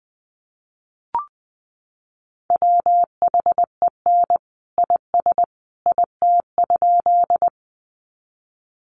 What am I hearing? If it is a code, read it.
Morse code, heard as a keyed tone, about 20 wpm, 700 Hz. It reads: WHEN IS IT?
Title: Sentences from Top 100 Words 20wpm